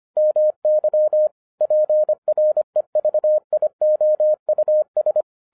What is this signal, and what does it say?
Morse code, heard as a keyed tone, about 24 wpm, 610 Hz. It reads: MY PREVIOUS
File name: 2011-10-mp3 / n04-JUST MOVED 25wpm